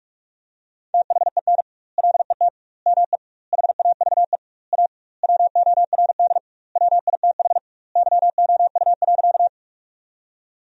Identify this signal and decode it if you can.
Morse code, heard as a keyed tone, about 41 wpm, 700 Hz. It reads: THEN LET ME HAVE A WORD WITH YOU.
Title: 2x Word-Spacing - Sentences from Top 300 Words - 45wpm